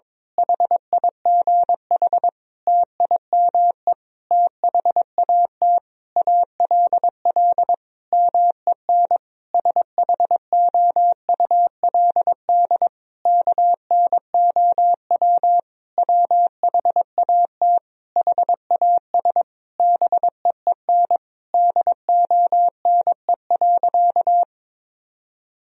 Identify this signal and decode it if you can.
Morse code, heard as a keyed tone, about 22 wpm, 700 Hz. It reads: HIGH TIME THAT ALL MEN SHOULD KNOW WHAT HAS BEEN DONE.
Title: Sentences from Top 1000 Words - Review Part 1 - 22wpm